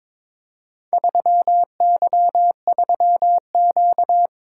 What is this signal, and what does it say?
Morse code, heard as a keyed tone, about 23 wpm, 700 Hz. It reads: 3Y3Q
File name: practice / 22wpm / Four Letter-Number Combinations - 22wpm